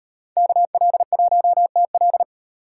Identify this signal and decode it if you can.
Morse code, heard as a keyed tone, about 31 wpm, 700 Hz. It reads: KL1TL